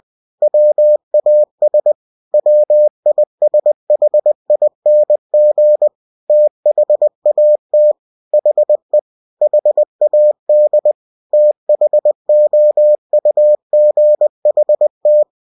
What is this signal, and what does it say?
Morse code, heard as a keyed tone, about 20 wpm, 600 Hz. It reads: WAS WISHING THAT HE HAD THOUGHT
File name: cw-books / winnie-the-pooh / winnie-20wpm-page-0034